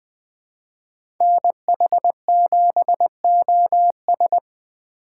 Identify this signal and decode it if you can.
Morse code, heard as a keyed tone, about 23 wpm, 700 Hz. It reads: NH7OS
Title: Speed Racing - USA 2x2 Call Signs 20wpm